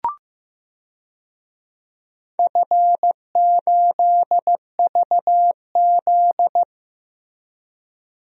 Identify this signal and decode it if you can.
Morse code, heard as a keyed tone, about 15 wpm, 700 Hz. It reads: F8VZ
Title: DXCC-100 Call Signs - 4 Characters - Repeated 3x - 15wpm